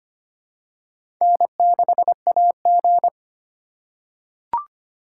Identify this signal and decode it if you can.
Morse code, heard as a keyed tone, about 25 wpm, 700 Hz. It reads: N6AG
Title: USA Call Signs - 4 Characters - 3x Character Spacing - 25wpm